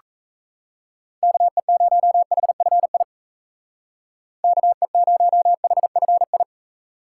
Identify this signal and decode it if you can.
Morse code, heard as a keyed tone, about 37 wpm, 700 Hz. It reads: KE0HFI KE0HFI